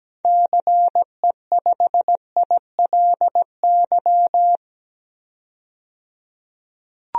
Morse code, heard as a keyed tone, about 17 wpm, 700 Hz. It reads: CE5ILY
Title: DXCC-100 Call Signs - 6 Characters - 17wpm